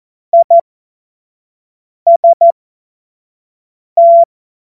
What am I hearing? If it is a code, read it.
Morse code, heard as a keyed tone, about 14 wpm, spaced as for 5 wpm, 680 Hz. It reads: IST
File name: Morsekurs / Witze / Witz 02 Januar 2024 25 F70